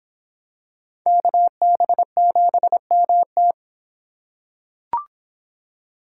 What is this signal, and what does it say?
Morse code, heard as a keyed tone, about 24 wpm, 700 Hz. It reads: KB7MT